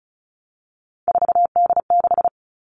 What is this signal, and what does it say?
Morse code, heard as a keyed tone, about 34 wpm, 700 Hz. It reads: 4D6
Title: All Cut Numbers - 3-Digit 35wpm